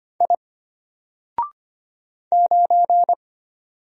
Morse code, heard as a keyed tone, about 25 wpm, 700 Hz. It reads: I 9
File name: Single-Letter-Number-25wpm